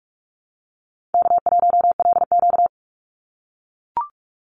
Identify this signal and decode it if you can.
Morse code, heard as a keyed tone, about 44 wpm, 700 Hz. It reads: K1LQ
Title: USA Call Signs 45wpm